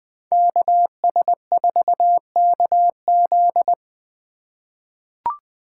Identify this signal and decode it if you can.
Morse code, heard as a keyed tone, about 20 wpm, 700 Hz. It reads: KS4KZ
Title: USA Call Signs - 4x Character Spacing - 20wpm